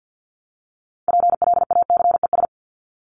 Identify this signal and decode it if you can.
Morse code, heard as a keyed tone, about 50 wpm, 700 Hz. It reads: PLACES
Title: ES:S Suffix - Top 100 Words with Base Words 50wpm